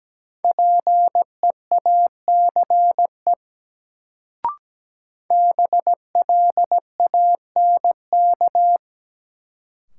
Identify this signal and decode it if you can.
Morse code, heard as a keyed tone, about 17 wpm, 700 Hz. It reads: PEACE BLANK
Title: Sentences from Top 1500 Words - New Word Drill Alt - 17wpm